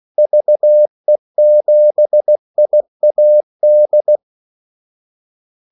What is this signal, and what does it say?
Morse code, heard as a keyed tone, about 16 wpm, 600 Hz. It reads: VE7IAD
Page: Kaldesignaler 16 wpm